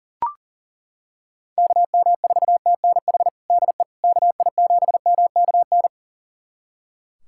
Morse code, heard as a keyed tone, about 40 wpm, 700 Hz. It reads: KM4TNH DE KI7MKN
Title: USA Callsign DE Callsign - 6 Characters - 40wpm